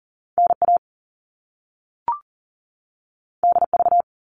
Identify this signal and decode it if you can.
Morse code, heard as a keyed tone, about 40 wpm, 700 Hz. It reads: NA DV
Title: All Cut Numbers - 2-Digit 40wpm